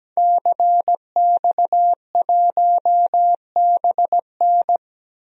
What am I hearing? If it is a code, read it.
Morse code, heard as a keyed tone, about 17 wpm, 700 Hz. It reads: CX1BN
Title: DXCC-100 Call Signs - 5 Characters - 17wpm